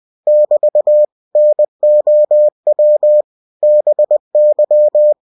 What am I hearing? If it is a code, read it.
Morse code, heard as a keyed tone, about 20 wpm, 600 Hz. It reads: <BT> NOW BY
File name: cw-books / winnie-the-pooh / winnie-20wpm-page-0016